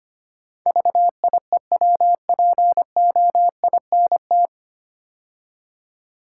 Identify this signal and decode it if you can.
Morse code, heard as a keyed tone, about 25 wpm, 700 Hz. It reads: VIEWPOINT
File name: practice / 25wpm / 9 Letter Words - First 3 Letters Given - 25wpm